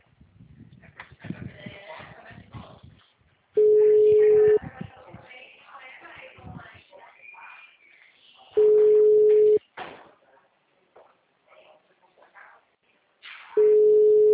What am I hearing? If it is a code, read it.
Ringback tone (Europe, 425 Hz).
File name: out- -2008- - - .826